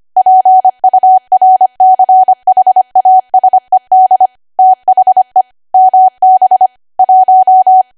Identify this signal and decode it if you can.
Morse code, heard as a keyed tone, about 25 wpm, 750 Hz. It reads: PURCHASED THE MB 1
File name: cwa / W1AW-Level3 / 25WPM